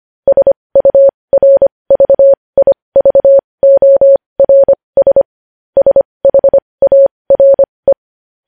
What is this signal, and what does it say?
Morse code, heard as a keyed tone, about 25 wpm, spaced as for 20 wpm, 550 Hz. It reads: SURVIVORS SHARE
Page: news-headlines-in-morse-code-at-20-wpm